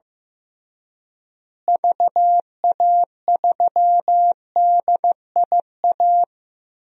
Morse code, heard as a keyed tone, about 15 wpm, 700 Hz. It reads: VA3DIA
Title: DXCC-100 Call Signs - 6 Characters - Repeated 2x - 15wpm